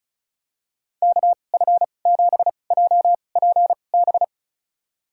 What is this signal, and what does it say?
Morse code, heard as a keyed tone, about 34 wpm, spaced as for 25 wpm, 700 Hz. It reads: KF7JPB
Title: USA Call Signs - 2x Character Spacing - 35wpm